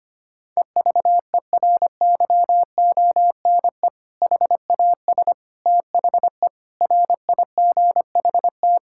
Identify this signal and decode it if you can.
Morse code, heard as a keyed tone, about 25 wpm, 700 Hz. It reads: EVERYONE HAS THE RIGHT
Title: Sentences from Top 1000 Words - Review Part 5 - 25wpm